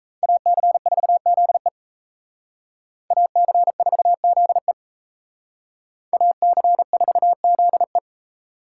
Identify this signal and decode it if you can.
Morse code, heard as a keyed tone, about 35 wpm, 700 Hz. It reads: AC4ZE AC4ZE AC4ZE